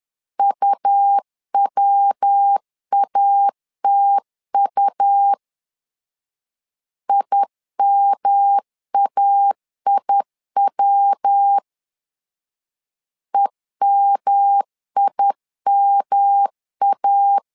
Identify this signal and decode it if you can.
Morse code, heard as a keyed tone, about 10 wpm, 800 Hz. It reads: UWATU IMAIW EMIMA
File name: LEZIONI / 3 MWRO / 3 MW 10wpm 1959